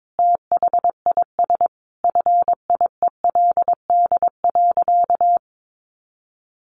Morse code, heard as a keyed tone, about 22 wpm, 700 Hz. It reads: THIS FIELD.
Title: Sentences from Top 1200 Words 22wpm